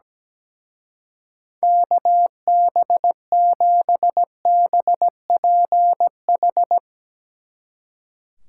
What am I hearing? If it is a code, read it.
Morse code, heard as a keyed tone, about 17 wpm, 700 Hz. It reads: KB7BPH
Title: Speed Racing - USA 2x3 Call Signs 17wpm